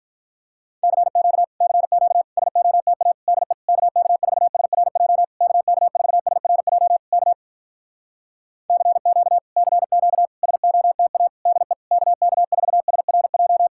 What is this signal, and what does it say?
Morse code, heard as a keyed tone, about 46 wpm, 700 Hz. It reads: CQ CQ SOTA DE KK4SRJ KK4SRJ K CQ CQ SOTA DE KK4SRJ